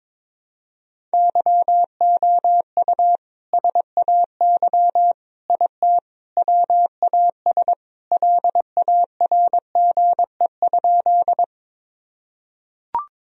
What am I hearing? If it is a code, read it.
Morse code, heard as a keyed tone, about 22 wpm, 700 Hz. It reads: YOU SAY IT WAS LARGE?
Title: Sentences from Top 1000 Words - Review Part 1 - 22wpm